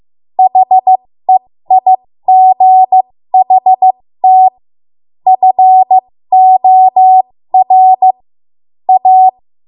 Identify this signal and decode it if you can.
Morse code, heard as a keyed tone, about 15 wpm, spaced as for 13 wpm, 750 Hz. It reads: HEIGHT FOR A
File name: Morse / 13WPM